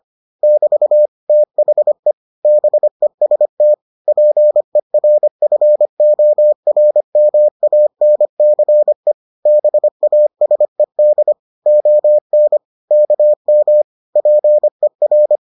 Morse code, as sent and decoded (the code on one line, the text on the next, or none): -...- / - .... / -... ... - / .--. .-. ..-. --- .-. -- .- -. -.-. / -... .- ... -.. / --- -. / -.- -- / .--. .-.
<BT> THE BEST PERFORMANCE BASED ON KM PER